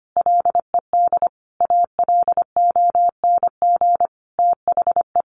Morse code, heard as a keyed tone, about 25 wpm, 700 Hz. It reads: LED ALONG THE
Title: A Sound Of Thunder 25wpm